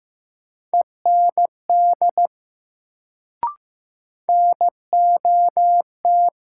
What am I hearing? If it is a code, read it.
Morse code, heard as a keyed tone, about 15 wpm, 700 Hz. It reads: END NOT
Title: Sets of 2 Words - New Word Drill - 15wpm